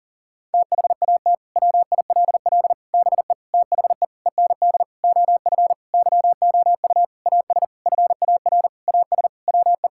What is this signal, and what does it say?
Morse code, heard as a keyed tone, about 40 wpm, 700 Hz. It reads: THAT WILL BE THE END OF YOU AS FAR AS WE
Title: Sentences from Top 1000 Words - Review Part 6 - 40wpm